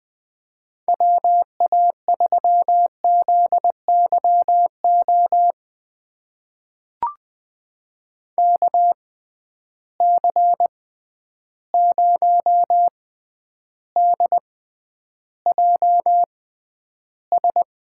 Morse code, heard as a keyed tone, about 21 wpm, spaced as for 10 wpm, 700 Hz. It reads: WA3ZYO KC0DJS
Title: USA Call Signs - 6 Characters - 6x Character Spacing - 20wpm